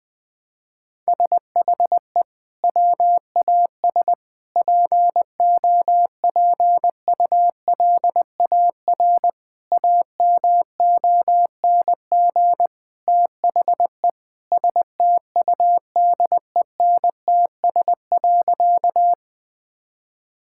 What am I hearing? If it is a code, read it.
Morse code, heard as a keyed tone, about 20 wpm, 700 Hz. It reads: SHE WAS POPULAR AMONG THE STUDENTS.
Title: Sentences with 2-Word Phrases - Spoken Beginning - 20wpm